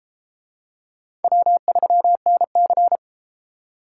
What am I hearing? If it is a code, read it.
Morse code, heard as a keyed tone, about 30 wpm, 700 Hz. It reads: W3NC